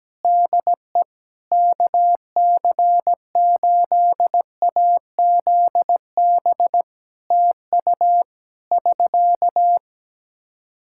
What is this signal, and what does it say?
Morse code, heard as a keyed tone, about 17 wpm, 700 Hz. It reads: DE KC8AZB TU <SK>
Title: Practice QSO - Exchange 4 - Ending - 17wpm